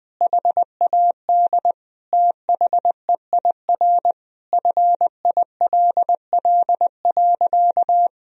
Morse code, heard as a keyed tone, about 20 wpm, 700 Hz. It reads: HAD THEIR FILL.
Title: Sentences from Top 1000 Words - Review Part 3 - 20wpm